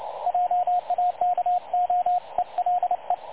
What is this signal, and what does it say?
Morse code, heard as a keyed tone, about 27 wpm, 720 Hz, with radio noise. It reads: JAKO ELE
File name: dlazaawansowanych / nagraniazpasma / websdr recording 3565.0kHz